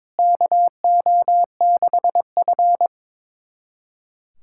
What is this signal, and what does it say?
Morse code, heard as a keyed tone, about 22 wpm, 700 Hz. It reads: KO6F
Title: USA Callsign DE Callsign - 4 Characters - 22wpm